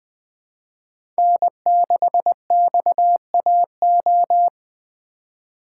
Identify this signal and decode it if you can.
Morse code, heard as a keyed tone, about 20 wpm, 700 Hz. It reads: N6XAO
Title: USA Call Signs - 3x Character Spacing - 20wpm